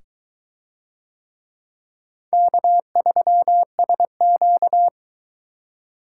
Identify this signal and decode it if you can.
Morse code, heard as a keyed tone, about 22 wpm, 700 Hz. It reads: K3SQ